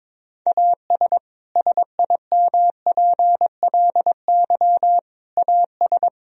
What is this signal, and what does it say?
Morse code, heard as a keyed tone, about 22 wpm, 700 Hz. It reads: AS SIMPLY AS
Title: Sentences from Top 1000 Words - Review Part 4 - 22wpm